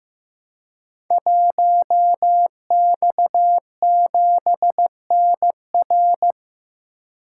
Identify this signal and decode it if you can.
Morse code, heard as a keyed tone, about 15 wpm, 700 Hz. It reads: 1X7NR
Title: Code Groups - 5 Letter:Numbers - 15wpm